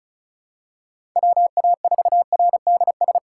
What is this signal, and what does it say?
Morse code, heard as a keyed tone, about 34 wpm, 700 Hz. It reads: WA4RDS
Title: USA Call Signs - 6 Characters - 8x Character Spacing - 35wpm